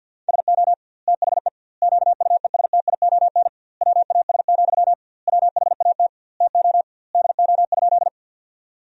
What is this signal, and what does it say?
Morse code, heard as a keyed tone, about 50 wpm, 700 Hz. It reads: SO THE QUESTION WAS, WHAT TO DO?
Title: 2x Word-Spacing - Sentences from Top 400 Words - 50wpm